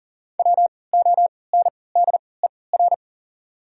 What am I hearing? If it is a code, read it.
Morse code, heard as a keyed tone, about 40 wpm, spaced as for 23 wpm, 700 Hz. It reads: WONDER
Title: 3x Character Spacing - Top 500 Words - 40wpm